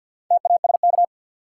Morse code, heard as a keyed tone, about 46 wpm, 700 Hz. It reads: TASK
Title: Sentences from Top 1000 Words - Review Part 7 - New Word Drill - 50wpm